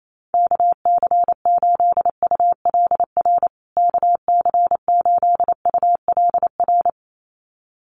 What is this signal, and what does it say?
Morse code, heard as a keyed tone, about 28 wpm, 700 Hz. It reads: KC8ULR KC8ULR
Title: USA Call Signs Repeated 2x 28wpm